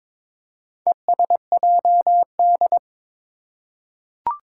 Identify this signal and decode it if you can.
Morse code, heard as a keyed tone, about 22 wpm, 700 Hz. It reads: ESJD
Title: Four Challenging Letter-Number Combinations 22wpm